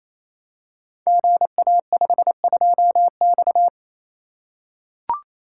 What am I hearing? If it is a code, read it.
Morse code, heard as a keyed tone, about 29 wpm, 700 Hz. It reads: GA52X
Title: Code Groups - 5 Letter:Numbers - 28wpm